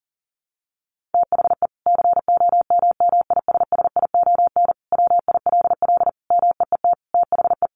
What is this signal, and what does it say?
Morse code, heard as a keyed tone, about 40 wpm, 700 Hz. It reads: THE COMMISSION WILL MEET THE
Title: Sentences from Top 1400 Words 40wpm